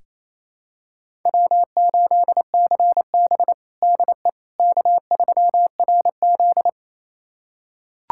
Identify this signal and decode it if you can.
Morse code, heard as a keyed tone, about 28 wpm, 700 Hz. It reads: W8CB DE K3RZ
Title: USA Callsign DE Callsign - 4 Characters - 28wpm